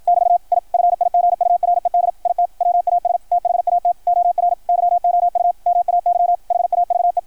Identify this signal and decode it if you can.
Morse code, heard as a keyed tone, about 53 wpm, 690 Hz, with radio noise. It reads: <BT> I FIGURED IT WAS THAT OR YOU MAY HAVE
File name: archived / CRSnet